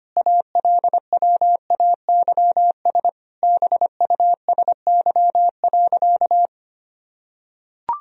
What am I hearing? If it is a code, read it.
Morse code, heard as a keyed tone, about 25 wpm, 700 Hz. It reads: ALWAYS BUSY.
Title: Sentences from Top 1000 Words - Review Part 3 - 25wpm